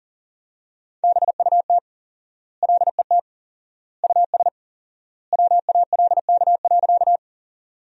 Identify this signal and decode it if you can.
Morse code, heard as a keyed tone, about 40 wpm, 700 Hz. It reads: BUT LET US WALK.
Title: 4x Word-Spacing - Sentences from Top 300 Words - 40wpm